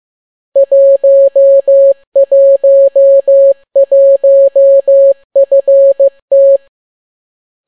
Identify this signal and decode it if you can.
Morse code, heard as a keyed tone, about 15 wpm, 550 Hz. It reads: 111FT